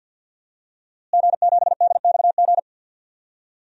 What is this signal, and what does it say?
Morse code, heard as a keyed tone, about 53 wpm, 700 Hz. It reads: G7DXG
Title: DXCC-100 Call Signs - 5 Characters - Repeated 3x - 50wpm